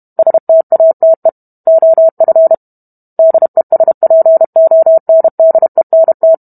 Morse code, heard as a keyed tone, about 32 wpm, 650 Hz. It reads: STATE OF DESPONDENT